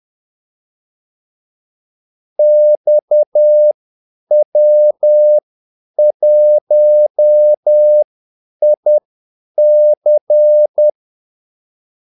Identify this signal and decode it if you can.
Morse code, heard as a keyed tone, about 10 wpm, spaced as for 8 wpm, 600 Hz. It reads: XW1IC
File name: Callsigns-10-8wpm